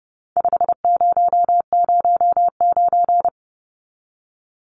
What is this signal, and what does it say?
Morse code, heard as a keyed tone, about 30 wpm, 700 Hz. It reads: 5009
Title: Numbers - 4-Digit 30wpm